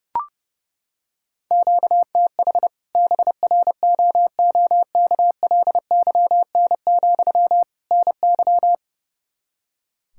Morse code, heard as a keyed tone, about 32 wpm, 700 Hz. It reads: QTH BROOKLYN, NY
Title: Practice QSO - Exchange 1 - Beginning - 30wpm